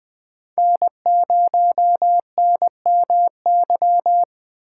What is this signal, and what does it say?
Morse code, heard as a keyed tone, about 20 wpm, 700 Hz. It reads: N0NMY
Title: USA Call Signs - 8x Character Spacing - 20wpm